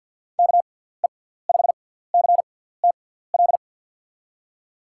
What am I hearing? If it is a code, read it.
Morse code, heard as a keyed tone, about 47 wpm, spaced as for 17 wpm, 700 Hz. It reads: KE5CTL
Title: USA Call Signs - 6x Character Spacing - 50wpm